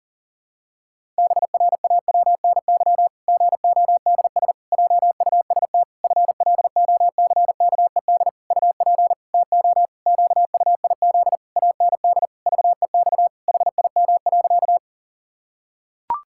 Morse code, heard as a keyed tone, about 40 wpm, 700 Hz. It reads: BRAWNY GODS JUST FLOCKED UP TO QUIZ AND VEX HIM.
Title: Pangram Challenge v2 - 40wpm